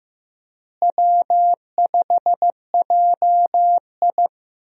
Morse code, heard as a keyed tone, about 14 wpm, 700 Hz. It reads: W5JI
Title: USA Call Signs - 4 Characters - Repeated 3x - 15wpm